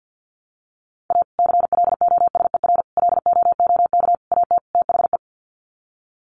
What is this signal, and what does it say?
Morse code, heard as a keyed tone, about 48 wpm, 700 Hz. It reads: A CLOSER LOOK AT THE
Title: Sets of 5 Words Encore 50wpm